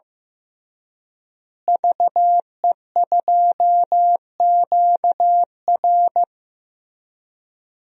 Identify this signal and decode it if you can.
Morse code, heard as a keyed tone, about 15 wpm, 700 Hz. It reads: VE2QR
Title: DXCC-100 Call Signs - 5 Characters - Repeated 3x - 15wpm